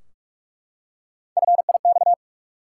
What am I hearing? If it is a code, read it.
Morse code, heard as a keyed tone, about 44 wpm, 700 Hz. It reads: FIX
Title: Three Letter Words - 45wpm